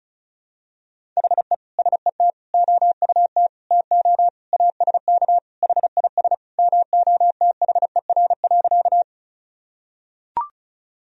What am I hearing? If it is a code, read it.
Morse code, heard as a keyed tone, about 34 wpm, 700 Hz. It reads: HE SET OUT TO ASK HIS MOTHER.
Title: Sentences from Top 1000 Words - Review Part 1 - 35wpm